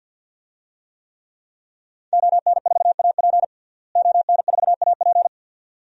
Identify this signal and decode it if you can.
Morse code, heard as a keyed tone, about 51 wpm, 700 Hz. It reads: ON4AP ON4AP
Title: DXCC-100 Call Signs - 5 Characters - Repeated 2x - 50wpm